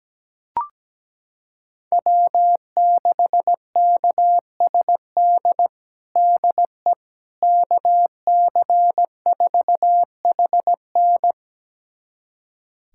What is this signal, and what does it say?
Morse code, heard as a keyed tone, about 17 wpm, 700 Hz. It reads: W6KSD DE KC4HN
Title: USA Callsign DE Callsign - 5 Characters - 17wpm